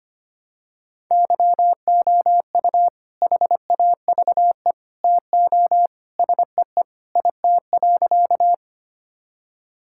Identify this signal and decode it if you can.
Morse code, heard as a keyed tone, about 25 wpm, 700 Hz. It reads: YOU HAVE TO SEE IT.
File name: practice / 25wpm / Sentences with 2-Word Phrases - 25wpm